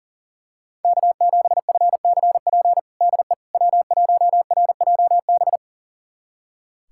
Morse code, heard as a keyed tone, about 40 wpm, 700 Hz. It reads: K7FCP DE W1RJB
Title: USA Callsign DE Callsign - 5 Characters - 40wpm